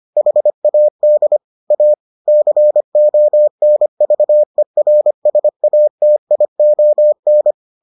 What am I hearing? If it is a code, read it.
Morse code, heard as a keyed tone, about 25 wpm, 600 Hz. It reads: HAD A CONVERSATION